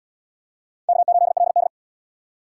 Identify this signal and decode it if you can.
Morse code, heard as a keyed tone, about 66 wpm, 700 Hz. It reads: K8LD